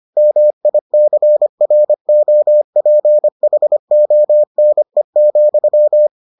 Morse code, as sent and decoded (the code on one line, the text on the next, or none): -- .. -.-. .-. --- .--. .... --- -. --..--
MICROPHONE,